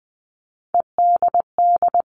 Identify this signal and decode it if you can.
Morse code, heard as a keyed tone, about 20 wpm, 700 Hz. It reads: EDD
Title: All Cut Numbers - 3-Digit 20wpm